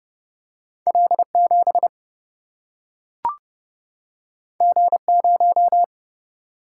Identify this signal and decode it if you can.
Morse code, heard as a keyed tone, about 30 wpm, 700 Hz. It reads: L7 G0
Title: Single-Letter Single-Number 30wpm